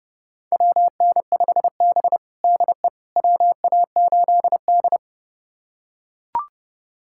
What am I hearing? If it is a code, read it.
Morse code, heard as a keyed tone, about 32 wpm, 700 Hz. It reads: WN5B DE WA8D
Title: USA Callsign DE Callsign - 4 Characters - 30wpm